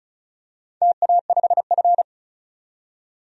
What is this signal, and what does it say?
Morse code, heard as a keyed tone, about 34 wpm, 700 Hz. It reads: TA5F
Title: DXCC-100 Call Signs - 4 Characters - 35wpm